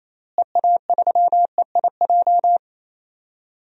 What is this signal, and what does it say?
Morse code, heard as a keyed tone, about 27 wpm, 700 Hz. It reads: EA3EIJ
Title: DXCC-100 Call Signs - 6 Characters - Repeated 3x - 28wpm